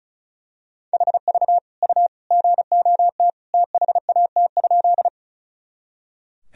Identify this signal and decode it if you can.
Morse code, heard as a keyed tone, about 36 wpm, 700 Hz. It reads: HV U GOT THAT?
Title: Sentences from Top 300 Words and Abbreviations 35wpm